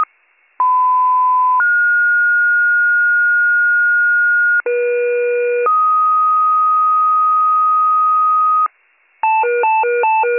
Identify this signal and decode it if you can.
Two-tone page, heard as a one-second tone, then a three-second tone, about 1010 then 1430 Hz, about 480 then 1250 Hz, with noise. It is heard from 0.6 to 8.7 s.